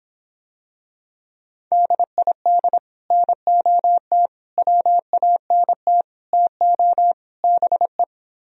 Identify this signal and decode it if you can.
Morse code, heard as a keyed tone, about 25 wpm, 700 Hz. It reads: DID NOT WANT TO BE